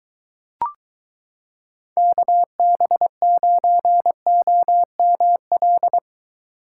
Morse code, heard as a keyed tone, about 23 wpm, 700 Hz. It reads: KB9OML